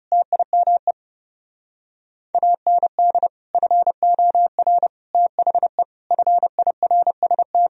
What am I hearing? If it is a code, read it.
Morse code, heard as a keyed tone, about 36 wpm, 700 Hz. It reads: TIME AND FOR THE FIRST
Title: Speed-Racing - Sets of 5 Words 30wpm